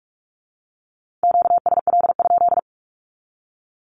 Morse code, heard as a keyed tone, about 46 wpm, 700 Hz. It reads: QSL?
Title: Q-Codes with Spoken Meaning - 45wpm